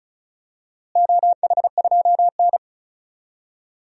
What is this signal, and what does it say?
Morse code, heard as a keyed tone, about 36 wpm, 700 Hz. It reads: OH2N
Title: DXCC-100 Call Signs - 4 Characters - Repeated 2x - 35wpm